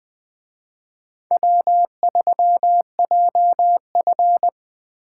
Morse code, heard as a keyed tone, about 19 wpm, 700 Hz. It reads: W3JF